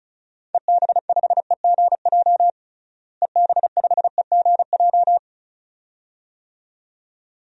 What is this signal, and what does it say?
Morse code, heard as a keyed tone, about 36 wpm, 700 Hz. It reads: EB5EGJ EB5EGJ
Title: DXCC-100 Call Signs - 6 Characters - Repeated 2x - 35wpm